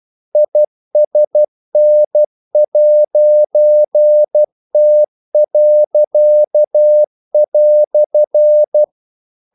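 Morse code, heard as a keyed tone, about 12 wpm, 600 Hz. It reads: ISN'T."
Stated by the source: Morse practice sending English quotations.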